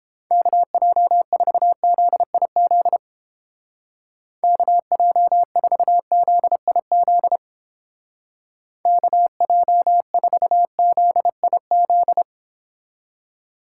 Morse code, heard as a keyed tone, about 31 wpm, 700 Hz. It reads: KJ4ZIZ KJ4ZIZ KJ4ZIZ